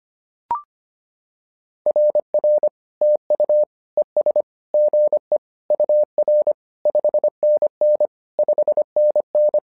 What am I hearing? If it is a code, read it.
Morse code, heard as a keyed tone, about 25 wpm, 600 Hz. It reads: RR TU ES GE UR 5NN 5NN
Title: SOTA QSO Practice 25wpm